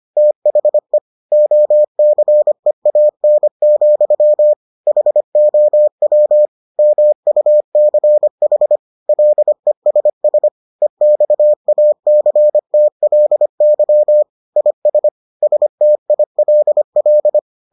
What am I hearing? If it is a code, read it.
Morse code, heard as a keyed tone, about 25 wpm, 600 Hz. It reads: THE OCEAN, HOW MUCH LESS EXACTLY IS STILL